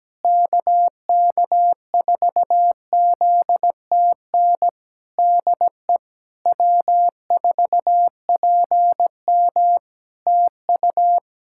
Morse code, heard as a keyed tone, about 17 wpm, 700 Hz. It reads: KK4ZTN DE W4PM TU
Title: Practice QSO - Exchange 4 - Ending - 17wpm